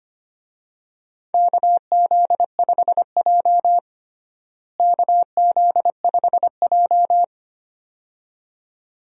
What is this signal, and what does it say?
Morse code, heard as a keyed tone, about 25 wpm, 700 Hz. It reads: KZ5J KZ5J
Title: USA Call Signs - 4 Characters - Repeated 2x - 25wpm